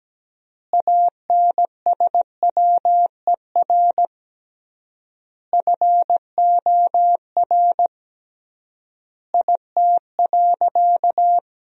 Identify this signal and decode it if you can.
Morse code, heard as a keyed tone, about 17 wpm, 700 Hz. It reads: ANSWER FOR IT.
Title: 3x Word-Spacing - Sentences from Top 300 Words - 17wpm